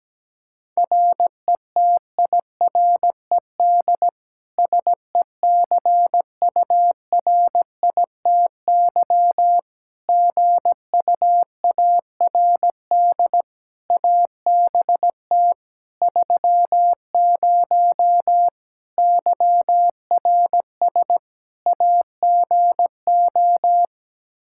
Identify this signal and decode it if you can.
Morse code, heard as a keyed tone, about 17 wpm, 700 Hz. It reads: RETIRED SECURITY GUARD ABT 30 YRS AGO
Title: K1USN SST QSO Practice 17wpm